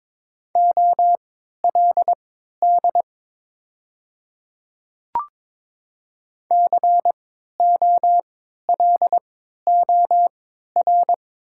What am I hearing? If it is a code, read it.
Morse code, heard as a keyed tone, about 22 wpm, spaced as for 13 wpm, 700 Hz. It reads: OLD COLOR
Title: 3x Character Spacing - Top 300 Words - 22wpm